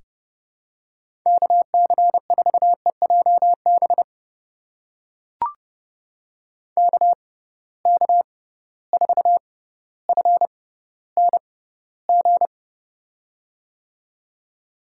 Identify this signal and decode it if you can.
Morse code, heard as a keyed tone, about 32 wpm, spaced as for 16 wpm, 700 Hz. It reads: KC4EJB KK4FNG
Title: USA Call Signs - 6 Characters - 6x Character Spacing - 30wpm